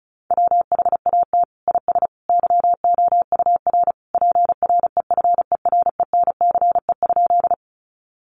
Morse code, heard as a keyed tone, about 34 wpm, 700 Hz. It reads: WHAT IS YOUR PREFERENCE?
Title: Sentences from Top 1500 Words 35wpm